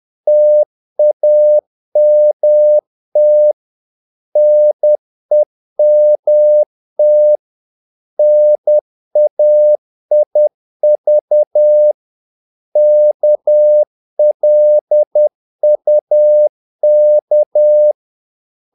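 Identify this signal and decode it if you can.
Morse code, heard as a keyed tone, about 10 wpm, 600 Hz. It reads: TAMT NEMT NAIV KLUK
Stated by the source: Morse practice of short Danish words at 10 wpm, from CW med Gnister.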